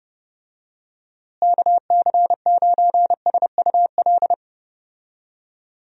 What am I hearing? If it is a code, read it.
Morse code, heard as a keyed tone, about 32 wpm, 700 Hz. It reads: KC9SUL
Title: USA Call Signs - 6 Characters - Repeated 2x - 30wpm